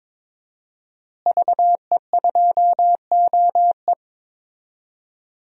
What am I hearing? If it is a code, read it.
Morse code, heard as a keyed tone, about 21 wpm, 700 Hz. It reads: VE2OE
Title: DXCC-100 Call Signs - 5 Characters - Repeated 2x - 22wpm